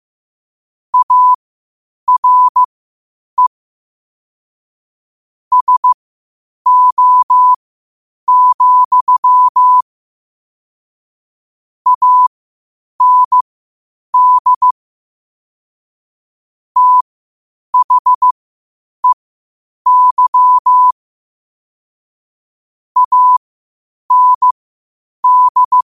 Morse code, heard as a keyed tone, about 15 wpm, spaced as for 8 wpm, 1000 Hz. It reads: ARE SO, AND THEY AND
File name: qotdfiles / 8wpm